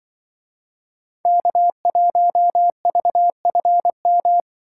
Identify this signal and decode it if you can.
Morse code, heard as a keyed tone, about 25 wpm, 700 Hz. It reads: K1VFM
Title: Speed Racing - USA 1x3 Call Signs 20wpm